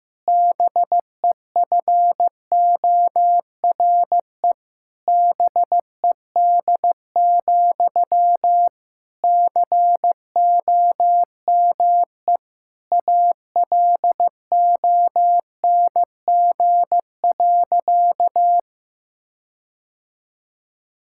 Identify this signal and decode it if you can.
Morse code, heard as a keyed tone, about 15 wpm, 700 Hz. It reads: BEFORE BED, COME ALONG.
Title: Sentences from Top 1600 Words 15wpm